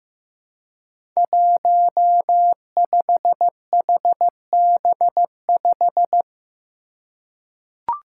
Morse code, heard as a keyed tone, about 15 wpm, 700 Hz. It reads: 15HB5
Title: Five Challenging Letter-Number Combinations 15wpm